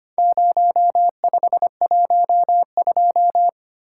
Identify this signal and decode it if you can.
Morse code, heard as a keyed tone, about 25 wpm, 700 Hz. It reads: 0512
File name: practice / 25wpm / Four Letter-Number Combinations - 25wpm